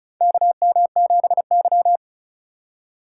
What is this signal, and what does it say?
Morse code, heard as a keyed tone, about 36 wpm, 700 Hz. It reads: KM7Y
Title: USA Call Signs - 4x Character Spacing - 35wpm